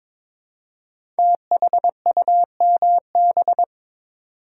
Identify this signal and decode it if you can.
Morse code, heard as a keyed tone, about 22 wpm, 700 Hz. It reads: THUMB
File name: 5 Letter Words - First 3 Letters Given - 22wpm